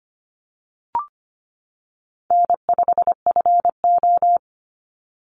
Morse code, heard as a keyed tone, about 25 wpm, 700 Hz. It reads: N5FO
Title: Easy Call Signs - 25wpm